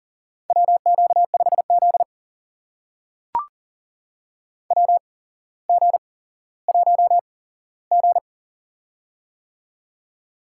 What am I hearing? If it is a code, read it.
Morse code, heard as a keyed tone, about 40 wpm, spaced as for 17 wpm, 700 Hz. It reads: WQ5Z WG1G
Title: USA Call Signs - 4 Characters - 8x Character Spacing - 40wpm